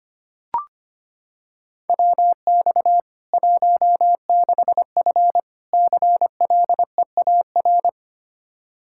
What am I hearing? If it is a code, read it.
Morse code, heard as a keyed tone, about 25 wpm, 700 Hz. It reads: WX 16F CLEAR
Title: K1USN SST QSO Practice 25wpm